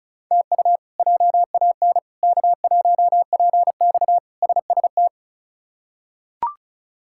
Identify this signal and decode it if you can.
Morse code, heard as a keyed tone, about 34 wpm, 700 Hz. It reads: TU JAN K1PX SST
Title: K1USN SST QSO Practice - 35wpm